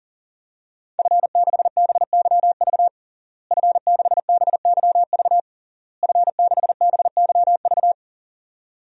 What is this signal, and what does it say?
Morse code, heard as a keyed tone, about 41 wpm, 700 Hz. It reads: F6BYV F6BYV F6BYV
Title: DXCC-100 Call Signs - 5 Characters - Repeated 3x - 40wpm